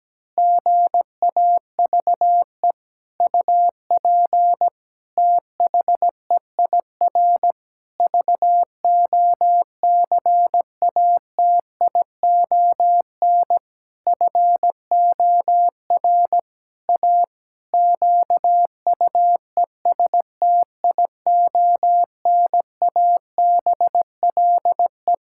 Morse code, heard as a keyed tone, about 17 wpm, 700 Hz. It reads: GAVE UP THEIR VOCATION FOR A QUESTIONABLE
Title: Pangram Challenge v2 - 17wpm